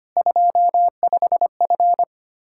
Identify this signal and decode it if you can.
Morse code, heard as a keyed tone, about 25 wpm, 700 Hz. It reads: 25F
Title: Practice QSO - Exchange 2 - Info - 25wpm